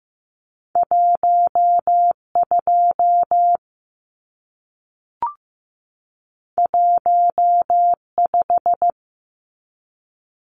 Morse code, heard as a keyed tone, about 15 wpm, 700 Hz. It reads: 12 15
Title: Numbers - 2-Digit 15wpm